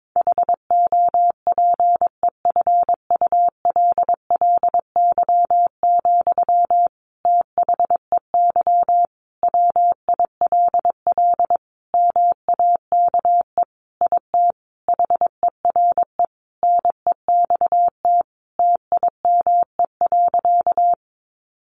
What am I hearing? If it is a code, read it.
Morse code, heard as a keyed tone, about 22 wpm, 700 Hz. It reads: HOPEFULLY, THEY WILL MAKE IT HERE NEXT TIME.
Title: Sentences from Top 1500 Words 22wpm